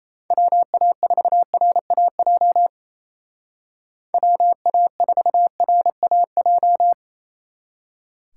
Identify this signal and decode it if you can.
Morse code, heard as a keyed tone, about 33 wpm, 700 Hz. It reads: WA4RAJ WA4RAJ
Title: Speed Racing - USA Call Signs 28wpm